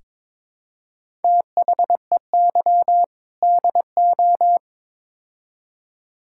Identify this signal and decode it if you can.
Morse code, heard as a keyed tone, about 22 wpm, 700 Hz. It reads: THEY DO
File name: practice / 22wpm / Build A Phrase - 5-Word Phrases - 22wpm